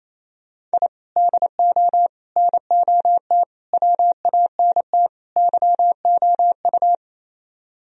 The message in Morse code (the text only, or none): I DO NOT WANT YOU